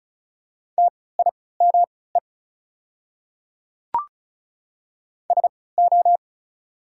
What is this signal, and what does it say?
Morse code, heard as a keyed tone, about 34 wpm, spaced as for 20 wpm, 690 Hz. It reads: TIME SO
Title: 3x Character Spacing - Top 100 Words - 35wpm